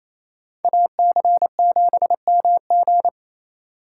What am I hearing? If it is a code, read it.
Morse code, heard as a keyed tone, about 28 wpm, 700 Hz. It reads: AC7MG
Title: USA Call Signs - 5 Characters - 2x Character Spacing - 28wpm